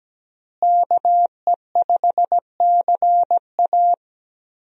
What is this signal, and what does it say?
Morse code, heard as a keyed tone, about 17 wpm, 700 Hz. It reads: KE5CA
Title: USA Call Signs - 5 Characters - Repeated 2x - 17wpm